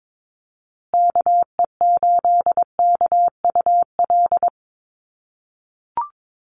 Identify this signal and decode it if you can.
Morse code, heard as a keyed tone, about 22 wpm, 700 Hz. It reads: KE8KUL
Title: USA Call Signs Repeated 3x 22wpm